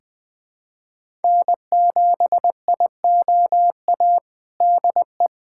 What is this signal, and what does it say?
Morse code, heard as a keyed tone, about 20 wpm, 700 Hz. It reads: N7IOA DE
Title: USA Callsign DE Callsign 20wpm